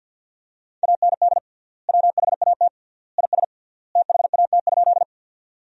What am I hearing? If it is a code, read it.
Morse code, heard as a keyed tone, about 50 wpm, 700 Hz. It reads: AND WHAT IS THAT?
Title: 3x Word-Spacing - Sentences from Top 100 Words - 50wpm